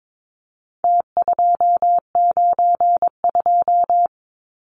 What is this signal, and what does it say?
Morse code, heard as a keyed tone, about 22 wpm, 700 Hz. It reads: T292
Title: Numbers - Contest Style - 4-Digit 22wpm